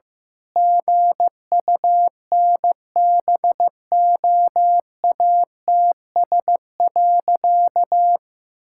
Morse code, heard as a keyed tone, about 15 wpm, 700 Hz. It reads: GUNBOATS.
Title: Pangram Challenge v2 - 15wpm